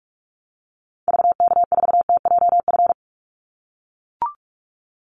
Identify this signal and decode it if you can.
Morse code, heard as a keyed tone, about 44 wpm, 700 Hz. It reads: VK4TJF
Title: Call Signs - 45wpm